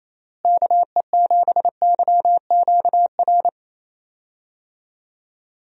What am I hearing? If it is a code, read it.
Morse code, heard as a keyed tone, about 28 wpm, 700 Hz. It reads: KE7YQR
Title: USA Call Signs - 6 Characters - Repeated 2x - 28wpm